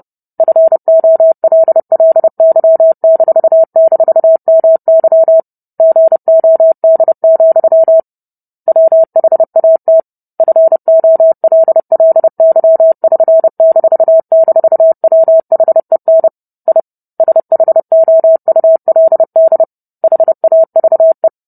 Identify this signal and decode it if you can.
Morse code, heard as a keyed tone, about 30 wpm, 650 Hz. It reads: FOLLY--MY GOD, WHAT FOLLY<SN>--WHEN I SHOULD HAVE